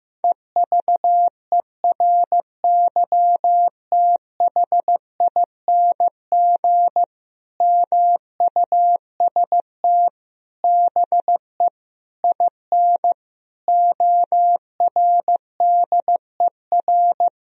Morse code, heard as a keyed tone, about 15 wpm, 700 Hz. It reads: EVERYTHING MUST BE IN ORDER
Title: Sentences from Top 1000 Words - Review Part 2 - 15wpm